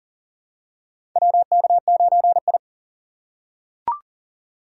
Morse code, heard as a keyed tone, about 39 wpm, 700 Hz. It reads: WK9I
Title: USA Call Signs - 4x Character Spacing - 40wpm